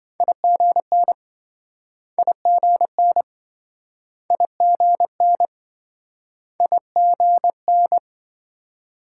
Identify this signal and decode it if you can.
Morse code, heard as a keyed tone, about 28 wpm, 700 Hz. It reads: IGN IGN IGN IGN